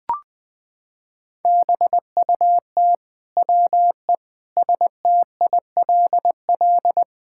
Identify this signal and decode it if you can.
Morse code, heard as a keyed tone, about 20 wpm, 700 Hz. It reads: BUT WE STILL
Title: Sentences with 9-Word Phrases - 20wpm